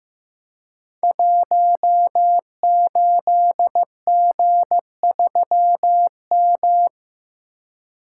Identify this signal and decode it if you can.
Morse code, heard as a keyed tone, about 15 wpm, 700 Hz. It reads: 18G3M
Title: Code Groups - 5 Letter:Numbers - 15wpm